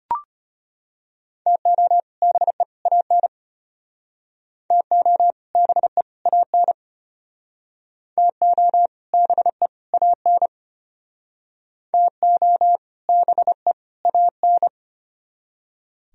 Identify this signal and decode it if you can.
Morse code, heard as a keyed tone, about 32 wpm, 700 Hz. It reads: TO BE AN TO BE AN TO BE AN TO BE AN